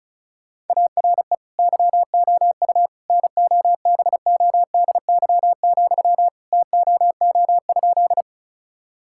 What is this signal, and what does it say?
Morse code, heard as a keyed tone, about 36 wpm, 700 Hz. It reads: ARE YOU NOBODY, TOO?
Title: Sentences from Top 1600 Words 35wpm